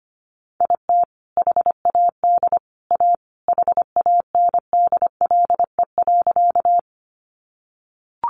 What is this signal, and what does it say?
Morse code, heard as a keyed tone, about 25 wpm, 700 Hz. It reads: IT HAD A HANDLE.
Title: Sentences from Top 1100 Words 25wpm